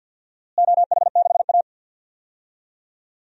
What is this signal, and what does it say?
Morse code, heard as a keyed tone, about 47 wpm, 700 Hz. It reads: OH6A
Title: DXCC-100 Call Signs - 4 Characters - Repeated 2x - 50wpm